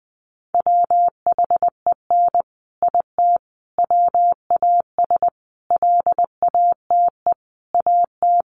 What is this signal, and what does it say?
Morse code, heard as a keyed tone, about 20 wpm, 700 Hz. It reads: WHEN IT WAS LATE AT
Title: Sentences from Top 600 Words 20wpm